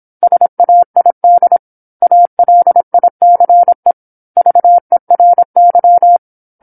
Morse code, heard as a keyed tone, about 27 wpm, 700 Hz. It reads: SAID ALICE VERY
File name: 08-Chapter-08 26 WPM